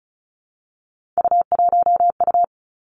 Morse code, heard as a keyed tone, about 36 wpm, 700 Hz. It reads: U1U
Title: Three Challenging Letter-Number Combinations 35wpm